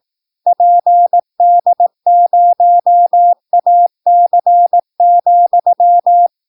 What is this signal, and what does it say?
Morse code, heard as a keyed tone, about 18 wpm, 700 Hz. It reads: PD0AC,
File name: PI4AA Uitzending